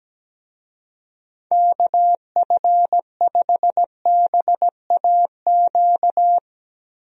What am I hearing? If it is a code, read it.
Morse code, heard as a keyed tone, about 17 wpm, 700 Hz. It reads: KF5BAQ
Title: USA Call Signs - 6 Characters - Repeated 2x - 17wpm